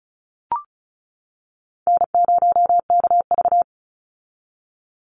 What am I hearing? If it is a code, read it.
Morse code, heard as a keyed tone, about 34 wpm, 700 Hz. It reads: N0KV
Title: Call Signs - 35wpm